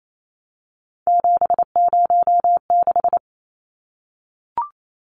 Morse code, heard as a keyed tone, about 28 wpm, 700 Hz. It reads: 706
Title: Numbers - 3-Digit 28wpm